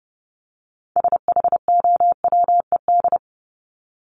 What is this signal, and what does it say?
Morse code, heard as a keyed tone, about 29 wpm, 700 Hz. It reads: SHOWED
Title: ED Suffix - Top 100 Words with Base Words 30wpm